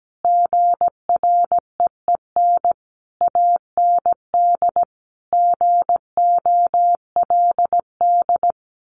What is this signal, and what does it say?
Morse code, heard as a keyed tone, about 17 wpm, 700 Hz. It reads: GREEN AND GOLD
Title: A Sound Of Thunder 17wpm